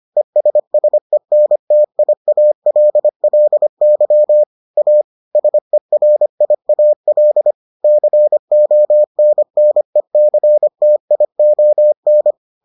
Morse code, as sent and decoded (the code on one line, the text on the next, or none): . ... ... -. - .. .- .-.. .-.. -.-- / .- / ... .-. .. .- .-.. / -.-. --- -. -. -.-. - .. --- -.
ESSENTIALLY A SERIAL CONNECTION